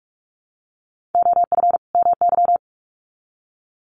Morse code, heard as a keyed tone, about 44 wpm, 700 Hz. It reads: OF MY
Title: Sets of 2 Words 45wpm